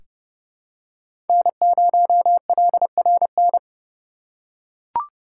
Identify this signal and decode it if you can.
Morse code, heard as a keyed tone, about 30 wpm, 700 Hz. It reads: N0LRN
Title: USA Call Signs - 5 Characters - 2x Character Spacing - 30wpm